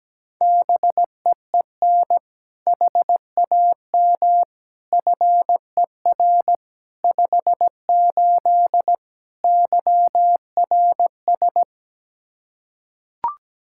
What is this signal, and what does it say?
Morse code, heard as a keyed tone, about 17 wpm, 700 Hz. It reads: BEEN HAM FER 58 YRS
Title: Practice QSO - Exchange 3 - More Info - 17wpm